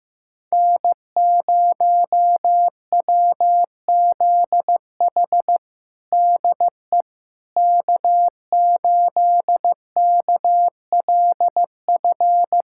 Morse code, heard as a keyed tone, about 15 wpm, 700 Hz. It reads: N0WZH DE K8KLF
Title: USA Callsign DE Callsign - 5 Characters - 15wpm